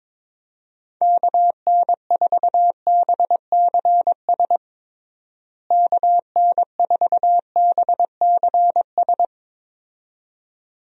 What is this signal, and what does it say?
Morse code, heard as a keyed tone, about 22 wpm, 700 Hz. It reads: KN4BCS KN4BCS
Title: USA Call Signs - 6 Characters - Repeated 2x - 22wpm